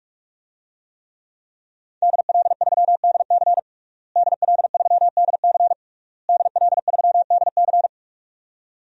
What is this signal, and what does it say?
Morse code, heard as a keyed tone, about 48 wpm, 700 Hz. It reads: DL3DC DL3DC DL3DC
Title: DXCC-100 Call Signs - 5 Characters - Repeated 3x - 45wpm